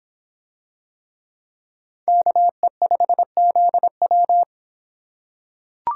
Morse code, heard as a keyed tone, about 26 wpm, 700 Hz. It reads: KE5ZW